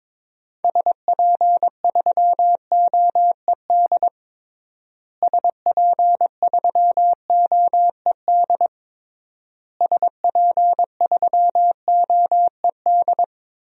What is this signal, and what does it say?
Morse code, heard as a keyed tone, about 22 wpm, 700 Hz. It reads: SP3OED SP3OED SP3OED
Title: DXCC-100 Call Signs - 6 Characters - Repeated 3x - 22wpm